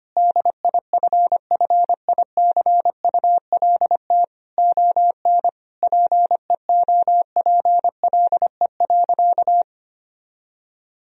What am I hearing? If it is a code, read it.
Morse code, heard as a keyed tone, about 25 wpm, 700 Hz. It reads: DIFFICULT ON PEOPLE.
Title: Sentences from Top 1000 Words - Review Part 3 - 25wpm